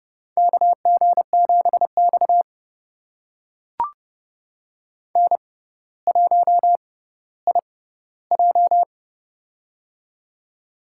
Morse code, heard as a keyed tone, about 31 wpm, spaced as for 15 wpm, 700 Hz. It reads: KG7X N1IJ
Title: USA Call Signs - 4 Characters - 6x Character Spacing - 30wpm